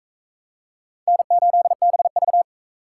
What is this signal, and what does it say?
Morse code, heard as a keyed tone, about 41 wpm, 700 Hz. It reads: N8BV